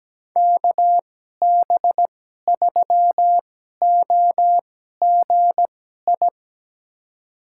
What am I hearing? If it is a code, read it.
Morse code, heard as a keyed tone, about 17 wpm, spaced as for 12 wpm, 700 Hz. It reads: KB3OGI
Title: USA Call Signs - 6 Characters - 2x Character Spacing - 17wpm